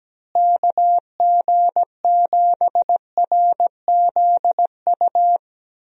Morse code, heard as a keyed tone, about 17 wpm, 700 Hz. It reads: KG7RZU
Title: USA Call Signs - 6 Characters - Repeated 3x - 17wpm